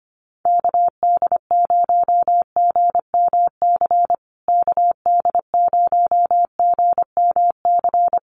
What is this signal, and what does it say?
Morse code, heard as a keyed tone, about 25 wpm, 700 Hz. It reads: KD0GMC KD0GMC
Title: USA Call Signs Repeated 3x 25wpm